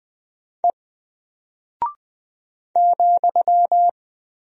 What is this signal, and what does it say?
Morse code, heard as a keyed tone, about 20 wpm, 700 Hz. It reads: E ,
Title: Single Character (Letter:Number:Punctuation:Prosign) Alt 20wpm